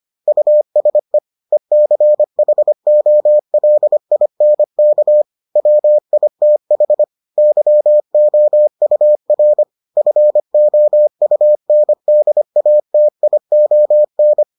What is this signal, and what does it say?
Morse code, heard as a keyed tone, about 25 wpm, 600 Hz. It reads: USE ECHOLINK WITH YOUR FOUNDATION